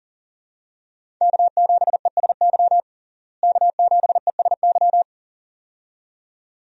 Morse code, heard as a keyed tone, about 40 wpm, 700 Hz. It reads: K7ESY K7ESY
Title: USA Call Signs - 5 Characters - Repeated 2x - 40wpm